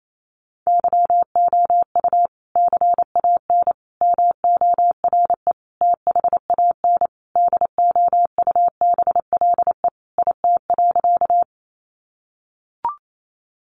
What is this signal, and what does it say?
Morse code, heard as a keyed tone, about 28 wpm, 700 Hz. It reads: YOU CAN MORE THAN DOUBLE IT.
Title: Sentences from Top 1400 Words 28wpm